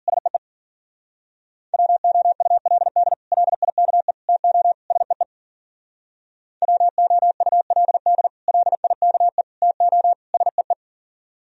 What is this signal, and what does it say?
Morse code, heard as a keyed tone, about 51 wpm, 700 Hz. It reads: SEE WOULD LIKE TO SEE WOULD LIKE TO SEE